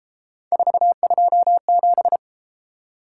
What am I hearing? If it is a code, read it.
Morse code, heard as a keyed tone, about 33 wpm, 700 Hz. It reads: 427